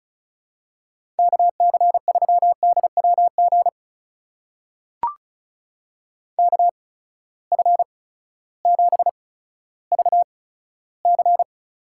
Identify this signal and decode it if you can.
Morse code, heard as a keyed tone, about 36 wpm, spaced as for 16 wpm, 700 Hz. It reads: KC3DWG KF7VC
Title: USA Call Signs - 8x Character Spacing - 35wpm